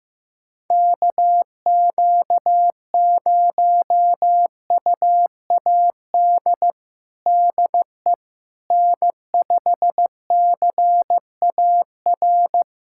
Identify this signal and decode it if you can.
Morse code, heard as a keyed tone, about 15 wpm, 700 Hz. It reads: KQ0UAD DE N5CAR
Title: USA Callsign DE Callsign 15wpm